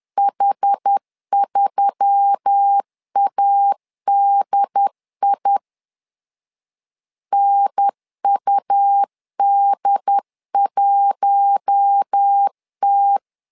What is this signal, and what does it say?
Morse code, heard as a keyed tone, about 10 wpm, 800 Hz. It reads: H3ADI NUD1T
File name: LEZIONI / 5 LBVC / 5 L 10wpm 2259